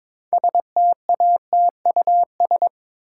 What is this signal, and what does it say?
Morse code, heard as a keyed tone, about 22 wpm, 700 Hz. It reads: STATUS
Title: Sentences from Top 1400 Words - New Word Drill Alt - 22wpm